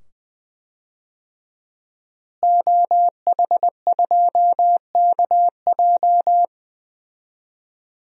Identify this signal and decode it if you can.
Morse code, heard as a keyed tone, about 20 wpm, 700 Hz. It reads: OH2KJ
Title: DXCC-100 Call Signs - 5 Characters - Repeated 2x - 20wpm